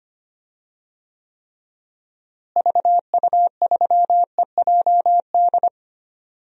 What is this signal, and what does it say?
Morse code, heard as a keyed tone, about 25 wpm, 700 Hz. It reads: VU3EJD
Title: DXCC-100 Call Signs - 6 Characters - Repeated 3x - 25wpm